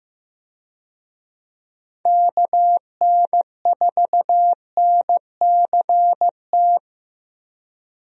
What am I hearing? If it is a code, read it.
Morse code, heard as a keyed tone, about 15 wpm, 700 Hz. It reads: KN4NCT
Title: USA Call Signs - 6 Characters - Repeated 2x - 15wpm